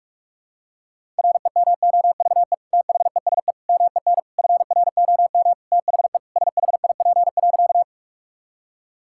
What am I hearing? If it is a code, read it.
Morse code, heard as a keyed tone, about 41 wpm, 690 Hz. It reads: REMOVE THESE MEN FROM THE SHIP.
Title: Sentences from Top 1000 Words - Review Part 5 - 45wpm